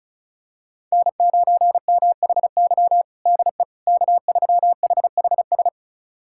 Morse code, heard as a keyed tone, about 34 wpm, 700 Hz. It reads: N9MHY DE K3HHS
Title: USA Callsign DE Callsign - 5 Characters - 35wpm